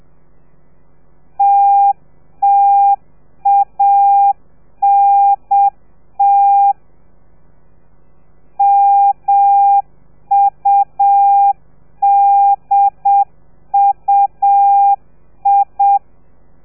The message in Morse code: TTANT MUDUI